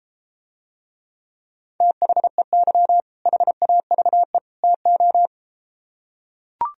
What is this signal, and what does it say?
Morse code, heard as a keyed tone, about 34 wpm, 700 Hz. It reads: THEY HAVE TO